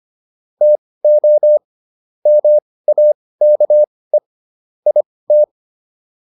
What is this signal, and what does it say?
Morse code, heard as a keyed tone, about 25 wpm, spaced as for 18 wpm, 600 Hz. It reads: TO MAKE IT
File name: WD204 18